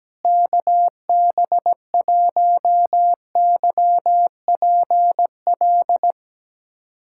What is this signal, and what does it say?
Morse code, heard as a keyed tone, about 17 wpm, 700 Hz. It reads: KB1YPL
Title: USA Callsign DE Callsign - 6 Characters - 17wpm